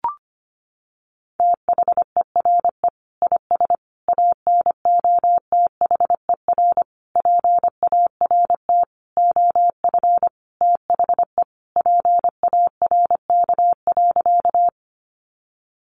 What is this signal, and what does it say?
Morse code, heard as a keyed tone, about 25 wpm, 700 Hz. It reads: THERE IS ANOTHER PART OF THE PARK.
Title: Sentences from Top 1300 Words 25wpm